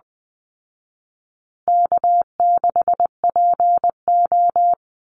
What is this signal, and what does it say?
Morse code, heard as a keyed tone, about 19 wpm, 700 Hz. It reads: K6PO
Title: Easy Call Signs Repeated 2x - 20wpm